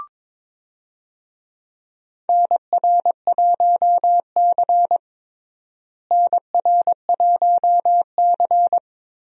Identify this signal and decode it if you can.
Morse code, heard as a keyed tone, about 22 wpm, 700 Hz. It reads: NR1C NR1C
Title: USA Call Signs - 4 Characters - Repeated 3x - 22wpm